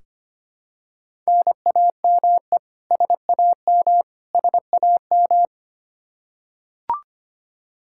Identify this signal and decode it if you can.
Morse code, heard as a keyed tone, about 25 wpm, 700 Hz. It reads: NAME SAM SAM
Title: Practice QSO - Exchange 1 - Beginning - 25wpm